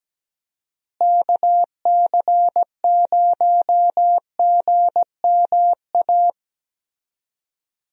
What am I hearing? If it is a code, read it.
Morse code, heard as a keyed tone, about 17 wpm, 700 Hz. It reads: KC0GMA
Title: USA Call Signs - 6 Characters - Repeated 2x - 17wpm